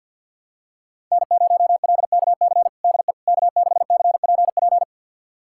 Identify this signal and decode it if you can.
Morse code, heard as a keyed tone, about 52 wpm, 700 Hz. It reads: N0LKC DE K6CPP
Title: USA Callsign DE Callsign - 5 Characters - 50wpm